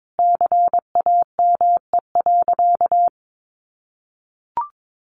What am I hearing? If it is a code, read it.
Morse code, heard as a keyed tone, about 22 wpm, 700 Hz. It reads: CAME.
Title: Sentences from Top 1500 Words 22wpm